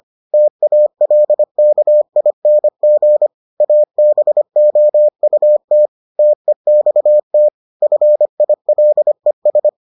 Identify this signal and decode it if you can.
Morse code, heard as a keyed tone, about 25 wpm, 600 Hz. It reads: TALKING ABOUT TEXT FILES